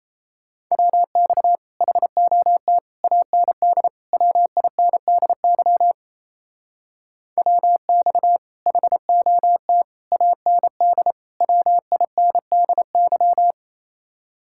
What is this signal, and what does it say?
Morse code, heard as a keyed tone, about 34 wpm, 700 Hz. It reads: WX HOT AND WINDY WX HOT AND WINDY
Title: Speed-Racing Common QSO Phrases Encore - 28wpm